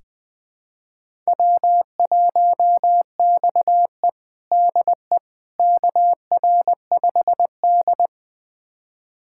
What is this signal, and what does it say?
Morse code, heard as a keyed tone, about 21 wpm, 700 Hz. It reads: W1XE DE KR5D
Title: USA Callsign DE Callsign - 4 Characters - 20wpm